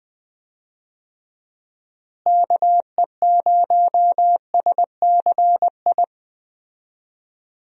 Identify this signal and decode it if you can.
Morse code, heard as a keyed tone, about 20 wpm, 700 Hz. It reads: KE0SCI
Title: USA Call Signs - 6 Characters - Repeated 2x - 20wpm